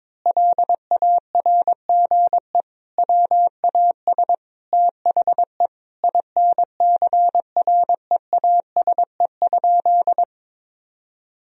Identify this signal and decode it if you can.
Morse code, heard as a keyed tone, about 22 wpm, 700 Hz. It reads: LARGE WAS THE INCREASE?
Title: Sentences from Top 1000 Words - Review Part 4 - 22wpm